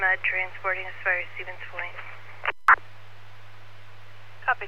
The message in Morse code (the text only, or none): none